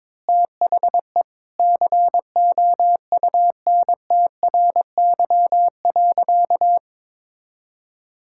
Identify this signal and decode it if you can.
Morse code, heard as a keyed tone, about 21 wpm, 700 Hz. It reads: THE COUNTRY.
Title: Sentences from Top 1000 Words - Review Part 5 - 22wpm